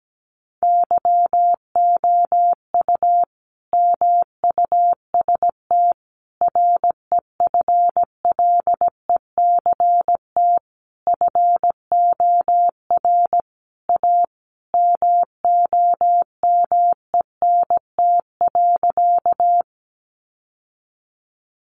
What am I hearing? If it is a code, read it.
Morse code, heard as a keyed tone, about 17 wpm, 700 Hz. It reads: YOU MUST REFLECT FOR A MOMENT.
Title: Sentences from Top 1500 Words 17wpm